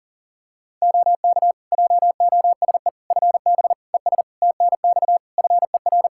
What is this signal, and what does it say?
Morse code, heard as a keyed tone, about 40 wpm, 700 Hz. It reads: OK JOSE FB ES TNX FER
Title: K1USN SST QSO Practice 40wpm